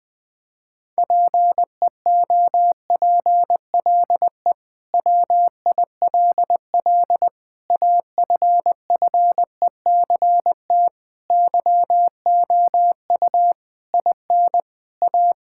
Morse code, heard as a keyed tone, about 21 wpm, 700 Hz. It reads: PEOPLE WILL AFFECT YOU IN A
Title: Sentences from Top 1000 Words - Review Part 6 - 20wpm